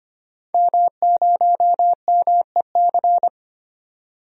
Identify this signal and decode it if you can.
Morse code, heard as a keyed tone, about 25 wpm, 700 Hz. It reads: M0MEC
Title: DXCC-100 Call Signs - 5 Characters - 25wpm